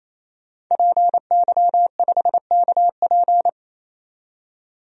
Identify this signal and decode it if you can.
Morse code, heard as a keyed tone, about 28 wpm, 700 Hz. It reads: PY5KP
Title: DXCC-100 Call Signs - 5 Characters - 28wpm